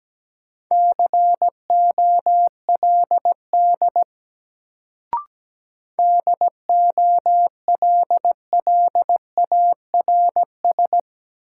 Morse code, heard as a keyed tone, about 17 wpm, 700 Hz. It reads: COLD DOLLARS
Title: Sets of 5 Words - New Word Drill - 17wpm